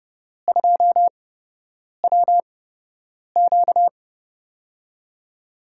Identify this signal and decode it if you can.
Morse code, heard as a keyed tone, about 30 wpm, spaced as for 8 wpm, 700 Hz. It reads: 2WQ
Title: USA Call Signs - 8x Character Spacing - 30wpm